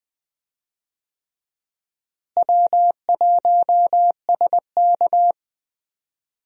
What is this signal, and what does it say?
Morse code, heard as a keyed tone, about 20 wpm, 700 Hz. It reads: W1SK
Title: USA Call Signs - 4 Characters - Repeated 3x - 20wpm